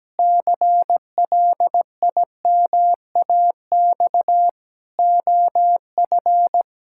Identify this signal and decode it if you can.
Morse code, heard as a keyed tone, about 17 wpm, 700 Hz. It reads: CLIMAX OF
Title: Pangram Challenge v2 - 17wpm